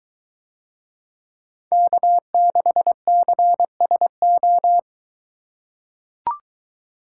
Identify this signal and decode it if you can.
Morse code, heard as a keyed tone, about 22 wpm, 700 Hz. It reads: K6CSO